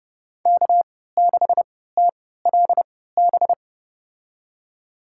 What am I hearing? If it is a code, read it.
Morse code, heard as a keyed tone, about 29 wpm, spaced as for 17 wpm, 700 Hz. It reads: K6TLB
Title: USA Call Signs - 5 Characters - 3x Character Spacing - 30wpm